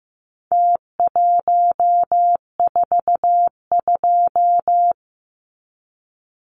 Morse code, heard as a keyed tone, about 15 wpm, 700 Hz. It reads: T142
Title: Numbers - Contest Style - 4-Digit 15wpm